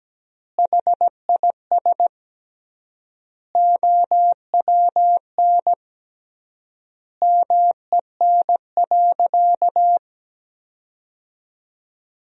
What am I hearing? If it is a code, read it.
Morse code, heard as a keyed tone, about 17 wpm, 700 Hz. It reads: HIS OWN MEN.
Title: 3x Word-Spacing - Sentences from Top 300 Words - 17wpm